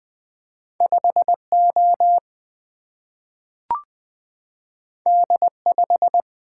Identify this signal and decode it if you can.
Morse code, heard as a keyed tone, about 20 wpm, 700 Hz. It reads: 5O D5
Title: Single-Letter Single-Number 20wpm